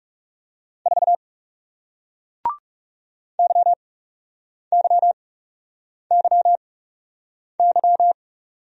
Morse code, heard as a keyed tone, about 40 wpm, spaced as for 8 wpm, 700 Hz. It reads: 4 YYYY